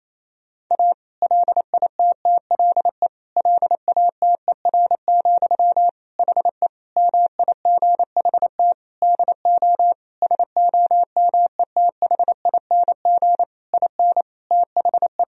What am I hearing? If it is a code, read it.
Morse code, heard as a keyed tone, about 27 wpm, 700 Hz. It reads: A LITTLE LATER, HE MIGHT DO SOMETHING IN THE
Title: Sentences from Top 1000 Words - Review Part 4 - 28wpm